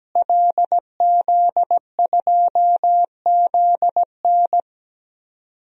Morse code, heard as a keyed tone, about 17 wpm, 700 Hz. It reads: LZ2ZN
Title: DXCC-100 Call Signs - 5 Characters - Repeated 3x - 17wpm